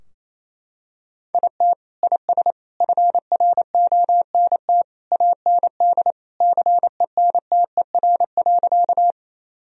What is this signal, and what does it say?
Morse code, heard as a keyed tone, about 28 wpm, 700 Hz. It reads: IT IS FRONT AND CENTER.
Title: Sentences from Top 1000 Words - Review Part 2 - 28wpm